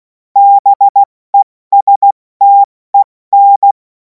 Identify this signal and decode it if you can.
Morse code, heard as a keyed tone, about 16 wpm, 800 Hz. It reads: BESTEN
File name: Morsekurs / Witze / Witz 11. August 2026 70 F80